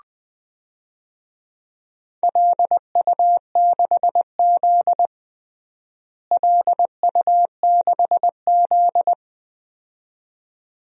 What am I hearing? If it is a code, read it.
Morse code, heard as a keyed tone, about 20 wpm, 700 Hz. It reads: LU6Z LU6Z
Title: DXCC-100 Call Signs - 4 Characters - Repeated 2x - 20wpm